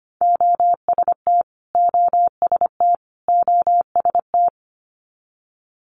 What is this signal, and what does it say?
Morse code, heard as a keyed tone, about 25 wpm, 700 Hz. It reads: OST OST OST
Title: Multi-letter Phonemes Repeated 3x 25wpm